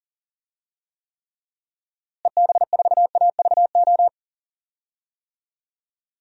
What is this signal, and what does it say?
Morse code, heard as a keyed tone, about 40 wpm, 700 Hz. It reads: EB4AVO
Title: DXCC-100 Call Signs - 6 Characters - Repeated 3x - 40wpm